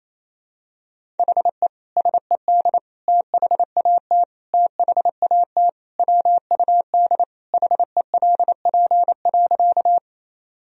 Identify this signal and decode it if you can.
Morse code, heard as a keyed tone, about 28 wpm, 700 Hz. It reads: HE SED THAT THAT WUD HELP.
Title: Sentences from Top 200 Words and Abbreviations 28wpm